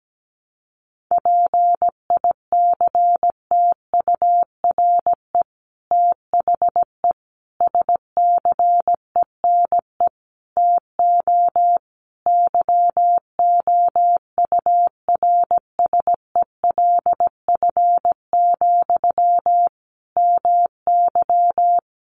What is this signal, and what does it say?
Morse code, heard as a keyed tone, about 17 wpm, 700 Hz. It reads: PICTURE THE SCENE TO YOURSELF, MY
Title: Sentences from Top 1100 Words 17wpm